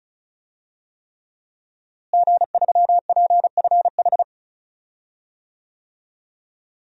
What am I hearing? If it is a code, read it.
Morse code, heard as a keyed tone, about 36 wpm, 700 Hz. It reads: G3PFH
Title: DXCC-100 Call Signs - 5 Characters - Repeated 3x - 35wpm